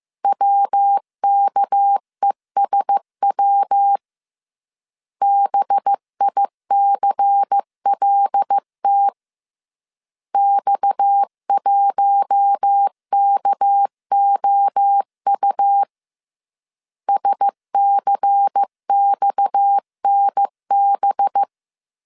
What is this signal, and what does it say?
Morse code, heard as a keyed tone, about 15 wpm, 800 Hz. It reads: WKESW BICLT X1KOU SCXNB
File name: CW / DET / 1 6 1500C S C 14wpm 2951